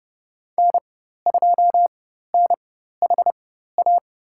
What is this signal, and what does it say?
Morse code, heard as a keyed tone, about 30 wpm, spaced as for 14 wpm, 700 Hz. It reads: N2NHA
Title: USA Call Signs - 5 Characters - 4x Character Spacing - 30wpm